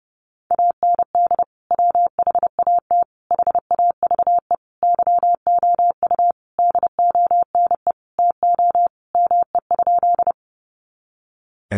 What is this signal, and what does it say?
Morse code, heard as a keyed tone, about 30 wpm, 700 Hz. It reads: AND WHAT HAVE YOU DONE TO ME?
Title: Sentences from Top 500 Words 30wpm